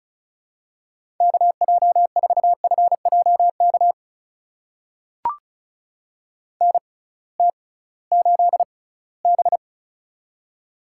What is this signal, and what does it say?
Morse code, heard as a keyed tone, about 36 wpm, spaced as for 21 wpm, 700 Hz. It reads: KJ4FJK NT8B
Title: USA Call Signs - 6x Character Spacing - 35wpm